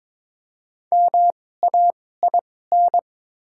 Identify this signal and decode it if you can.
Morse code, heard as a keyed tone, about 22 wpm, spaced as for 16 wpm, 700 Hz. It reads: MAIN
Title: 2x Character Spacing - Top 500 Words - 22wpm